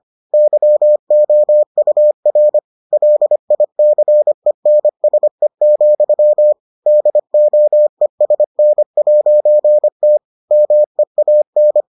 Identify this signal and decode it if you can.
Morse code, heard as a keyed tone, about 25 wpm, 600 Hz. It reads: YOUR LICENSE, DOESN'T MEAN